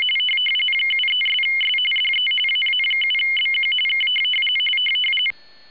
RTTY, about 45.45 baud. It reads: 73'S DE AA7BQ K